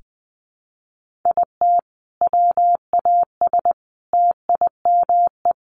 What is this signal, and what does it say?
Morse code, heard as a keyed tone, about 20 wpm, 700 Hz. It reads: IT WAS TIME
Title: Sentences from Top 1100 Words 20wpm